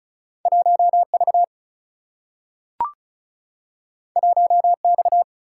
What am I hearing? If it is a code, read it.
Morse code, heard as a keyed tone, about 36 wpm, 700 Hz. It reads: 1 V 1 X
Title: Single-Letter Single-Number 35wpm